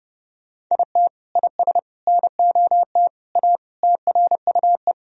Morse code, heard as a keyed tone, about 30 wpm, 700 Hz. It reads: IT IS NOT A TRUE
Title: Sentences from Top 1000 Words - Review Part 4 - 30wpm